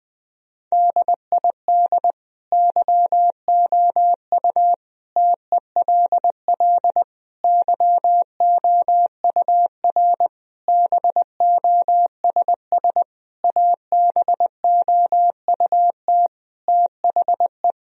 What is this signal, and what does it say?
Morse code, heard as a keyed tone, about 21 wpm, 700 Hz. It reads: DID YOU TELL YOUR BOSS ABOUT THE
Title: Sentences from Top 1000 Words - Review Part 4 - 20wpm